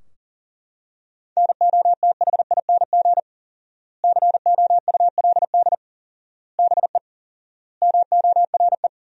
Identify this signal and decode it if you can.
Morse code, heard as a keyed tone, about 37 wpm, 700 Hz. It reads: NOTHING COULD BE MORE
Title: 4x Word-Spacing - Sentences from Top 500 Words - 40wpm